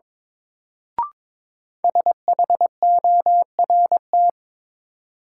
Morse code, heard as a keyed tone, about 22 wpm, 700 Hz. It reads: SHORT
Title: Sets of 5 Words Encore - New Word Drill - 22wpm